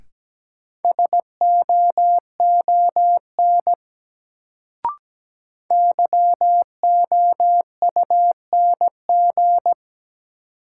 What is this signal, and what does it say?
Morse code, heard as a keyed tone, about 17 wpm, 700 Hz. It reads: SOON YOUNG
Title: Sentences from Top 400 Words - New Word Drill - 17wpm